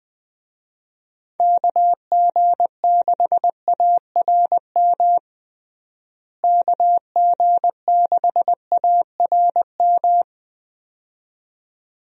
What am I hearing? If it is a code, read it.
Morse code, heard as a keyed tone, about 20 wpm, 700 Hz. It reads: KG6ARM KG6ARM
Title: USA Call Signs - 6 Characters - Repeated 2x - 20wpm